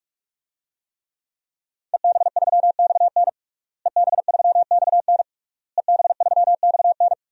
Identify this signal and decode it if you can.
Morse code, heard as a keyed tone, about 46 wpm, 700 Hz. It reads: EB3XN EB3XN EB3XN
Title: DXCC-100 Call Signs - 5 Characters - Repeated 3x - 45wpm